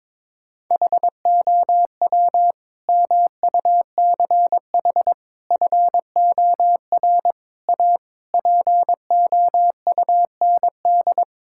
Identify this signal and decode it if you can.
Morse code, heard as a keyed tone, about 22 wpm, 700 Hz. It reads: HOW MUCH FOR A POUND
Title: Sentences from Top 1000 Words - Review Part 5 - 22wpm